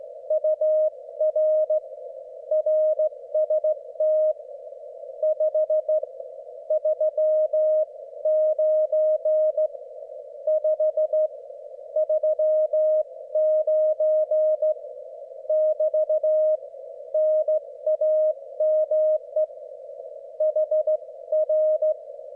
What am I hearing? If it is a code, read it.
Morse code, heard as a keyed tone, about 14 wpm, spaced as for 12 wpm, 620 Hz. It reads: UR RST 539 539 <BT> NAME HR
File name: Morsekurs / Witze / QSO W4 HB9DAL und HB9BPH